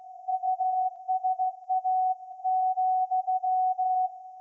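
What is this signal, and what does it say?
Morse code, heard as a keyed tone, about 15 wpm, 750 Hz. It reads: USA,
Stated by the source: Morse code from SAQ, received at a station in Putignano, Apulia, Italy, on VLF.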